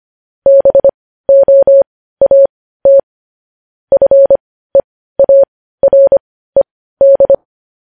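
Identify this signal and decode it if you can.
Morse code, heard as a keyed tone, about 25 wpm, spaced as for 15 wpm, 550 Hz. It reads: BOAT FEARED
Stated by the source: Morse code news headlines sent at 15, preceded by vvv.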